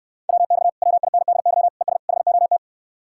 Morse code, heard as a keyed tone, about 69 wpm, 700 Hz. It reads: WX WINDY ES HOT